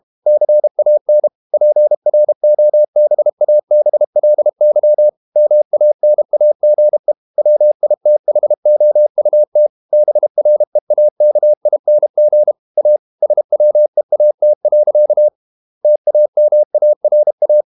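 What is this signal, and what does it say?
Morse code, heard as a keyed tone, about 32 wpm, 600 Hz. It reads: CAN PROBABLY MANAGE WITHOUT BREAKING A SWEAT. TAMARA
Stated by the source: Morse code practice at 32 wpm from CW med Gnister, reading Ken Follett's novel Never in English.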